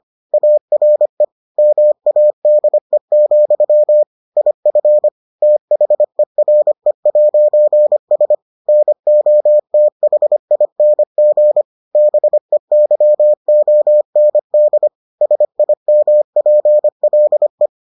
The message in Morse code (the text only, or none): ARE MADE, IF THERE'S NOTHING BEYOND SIMPLE